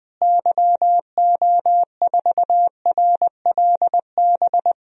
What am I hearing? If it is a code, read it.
Morse code, heard as a keyed tone, about 20 wpm, 700 Hz. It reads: YO4RLB
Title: DXCC-100 Call Signs - 6 Characters - Repeated 2x - 20wpm